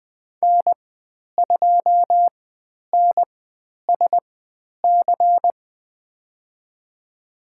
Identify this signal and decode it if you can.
Morse code, heard as a keyed tone, about 20 wpm, spaced as for 10 wpm, 700 Hz. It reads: N2NSC
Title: 109 Lesson 18g - Letter m - Call Signs 20wpm